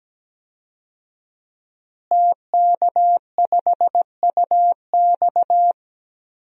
Morse code, heard as a keyed tone, about 17 wpm, 700 Hz. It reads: TK5UX
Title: DXCC-100 Call Signs - 5 Characters - Repeated 3x - 17wpm